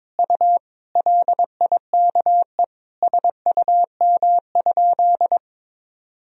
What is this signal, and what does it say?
Morse code, heard as a keyed tone, about 22 wpm, 700 Hz. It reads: U LIKE SUM?
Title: Sentences from Top 100 Words and Abbreviations 22wpm